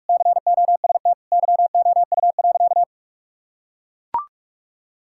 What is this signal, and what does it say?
Morse code, heard as a keyed tone, about 42 wpm, 700 Hz. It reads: COST YOU.
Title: Sentences from Top 1000 Words - Review Part 5 - 45wpm